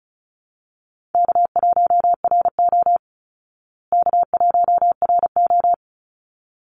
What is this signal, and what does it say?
Morse code, heard as a keyed tone, about 34 wpm, 700 Hz. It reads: K1RO K1RO
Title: Easy Call Signs Repeated 2x - 35wpm